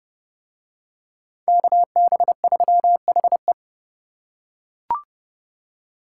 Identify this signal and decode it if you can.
Morse code, heard as a keyed tone, about 29 wpm, 700 Hz. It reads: KB3HE